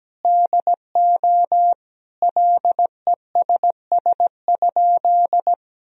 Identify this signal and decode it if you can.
Morse code, heard as a keyed tone, about 17 wpm, 700 Hz. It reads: DO LESS?
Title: Sentences from Top 1000 Words - Review Part 2 - 17wpm